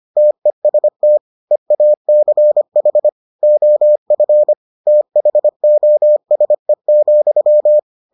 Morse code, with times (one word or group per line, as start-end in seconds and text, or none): 0.2-1.2 TEST
1.5-3.1 EACH
3.4-4.5 OF
4.9-7.8 THOSE,